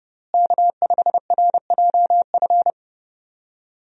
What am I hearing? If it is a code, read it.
Morse code, heard as a keyed tone, about 32 wpm, 700 Hz. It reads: K5RJF
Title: USA Call Signs - 5 Characters - Repeated 2x - 30wpm